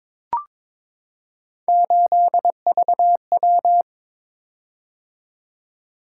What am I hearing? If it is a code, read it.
Morse code, heard as a keyed tone, about 22 wpm, 700 Hz. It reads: 8VW
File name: practice / 22wpm / Three Letter-Number Combinations - 22wpm